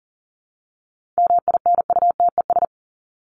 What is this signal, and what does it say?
Morse code, heard as a keyed tone, about 40 wpm, 700 Hz. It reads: MINUTES
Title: ES:S Suffix - Top 100 Words with Base Words 40wpm